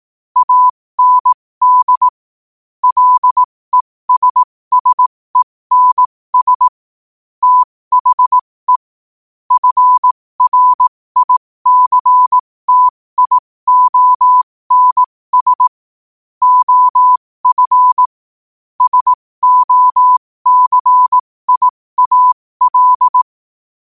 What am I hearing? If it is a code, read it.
Morse code, heard as a keyed tone, about 18 wpm, spaced as for 15 wpm, 1000 Hz. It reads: AND LESSENS THE FRICTIONS OF SOCIAL
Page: New quotes every day in morse code at 15 Words per minute.